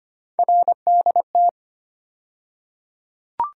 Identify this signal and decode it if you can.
Morse code, heard as a keyed tone, about 25 wpm, 700 Hz. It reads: RDT
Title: USA Call Sign Postfixes - 3 Letters 25wpm